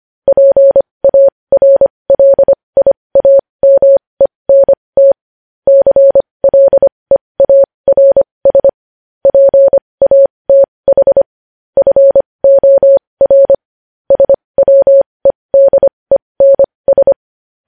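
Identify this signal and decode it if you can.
Morse code, heard as a keyed tone, about 25 wpm, spaced as for 20 wpm, 550 Hz. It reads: PARLIAMENT CLEARS PATH FOR SWEDENS
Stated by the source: Morse code at 20 wpm.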